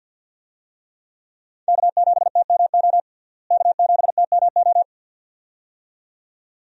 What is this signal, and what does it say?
Morse code, heard as a keyed tone, about 53 wpm, 700 Hz. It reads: K7TMO K7TMO
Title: USA Call Signs - 5 Characters - Repeated 2x - 50wpm